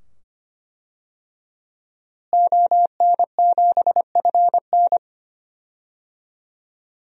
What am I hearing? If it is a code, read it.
Morse code, heard as a keyed tone, about 25 wpm, 700 Hz. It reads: ON7FN
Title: DXCC-100 Call Signs - 5 Characters - Repeated 2x - 25wpm